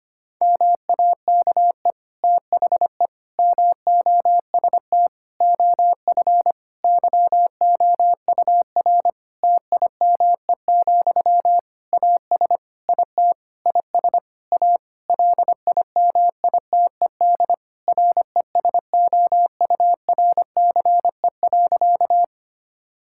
Morse code, heard as a keyed tone, about 25 wpm, 700 Hz. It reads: MAKE THE MOST OF YOUR TIME, AS IT IS A LIMITED RESOURCE.
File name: practice / 25wpm / Difficult Sentences with Hints - 25wpm